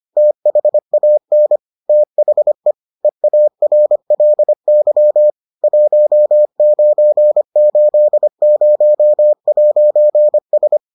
Morse code, heard as a keyed tone, about 25 wpm, 600 Hz. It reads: THAN THE EARLY 1980'S